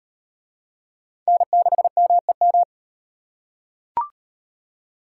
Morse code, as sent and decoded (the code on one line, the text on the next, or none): -. -.... -- . --
N6MEM